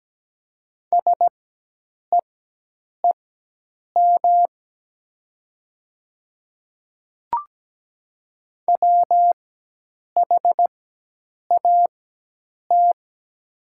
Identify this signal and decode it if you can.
Morse code, heard as a keyed tone, about 17 wpm, spaced as for 8 wpm, 700 Hz. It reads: SEEM WHAT
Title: 4x Character Spacing - Top 300 Words - 17wpm